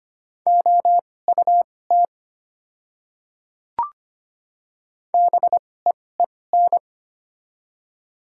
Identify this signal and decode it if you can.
Morse code, heard as a keyed tone, about 25 wpm, spaced as for 18 wpm, 690 Hz. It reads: OUT BEEN
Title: 2x Character Spacing - Top 200 Words - 25wpm